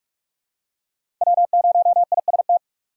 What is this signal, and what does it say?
Morse code, heard as a keyed tone, about 48 wpm, 700 Hz. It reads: W0IST
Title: USA Call Signs - 5 Characters - 6x Character Spacing - 45wpm